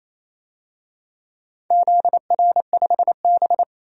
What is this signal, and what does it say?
Morse code, heard as a keyed tone, about 28 wpm, 700 Hz. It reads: ZR5B
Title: DXCC-100 Call Signs - 4 Characters - 28wpm